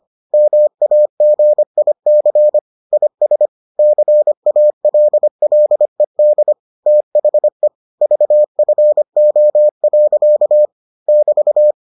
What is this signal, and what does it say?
Morse code, heard as a keyed tone, about 25 wpm, 600 Hz. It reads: MAGIC IS CALLED THE VFO. <BT>